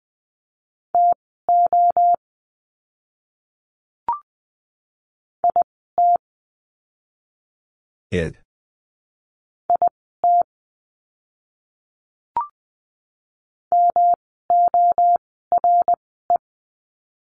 Morse code, heard as a keyed tone, about 20 wpm, spaced as for 15 wpm, 700 Hz. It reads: TO IT IT MORE
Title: 192 Lesson 31f - Letter v - Words 20wpm